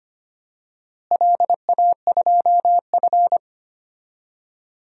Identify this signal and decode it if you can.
Morse code, heard as a keyed tone, about 25 wpm, 700 Hz. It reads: LA2F
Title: DXCC-100 Call Signs - 4 Characters - Repeated 2x - 25wpm